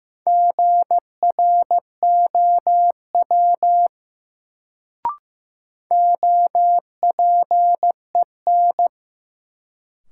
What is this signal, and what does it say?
Morse code, heard as a keyed tone, about 15 wpm, 700 Hz. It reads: GROW OPEN
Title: Sentences from Top 300 Words - New Word Drill - 15wpm